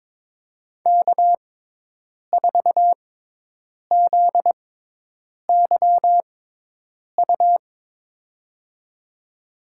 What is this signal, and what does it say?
Morse code, heard as a keyed tone, about 22 wpm, spaced as for 8 wpm, 700 Hz. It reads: K4ZYU
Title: USA Call Signs - 5 Characters - 6x Character Spacing - 22wpm